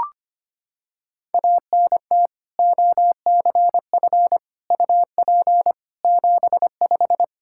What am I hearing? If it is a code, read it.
Morse code, heard as a keyed tone, about 25 wpm, 700 Hz. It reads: ANT OCF UP 75
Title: Practice QSO - Exchange 2 - Info - 25wpm